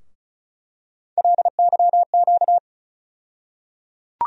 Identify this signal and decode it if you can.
Morse code, heard as a keyed tone, about 36 wpm, 700 Hz. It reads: LYQ
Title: USA Call Sign Postfixes - 3 Letters 35wpm